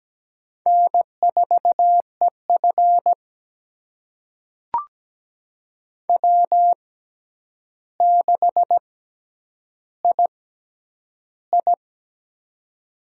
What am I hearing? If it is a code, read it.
Morse code, heard as a keyed tone, about 17 wpm, spaced as for 9 wpm, 700 Hz. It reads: N4EF W6II
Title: USA Call Signs - 4 Characters - 6x Character Spacing - 17wpm